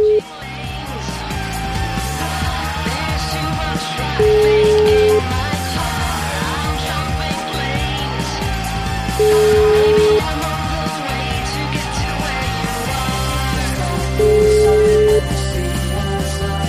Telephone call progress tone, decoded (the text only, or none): ringback tone